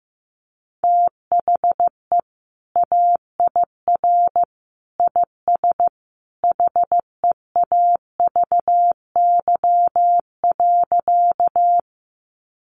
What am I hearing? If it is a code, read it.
Morse code, heard as a keyed tone, about 15 wpm, 700 Hz. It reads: THE AIR IS HEAVY.
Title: Sentences from Top 1100 Words 15wpm